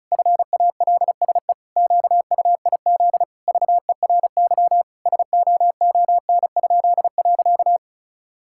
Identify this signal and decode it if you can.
Morse code, heard as a keyed tone, about 36 wpm, 700 Hz. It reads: FALSE QUIZ VERY SOON?.
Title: Pangram Challenge v2 - 35wpm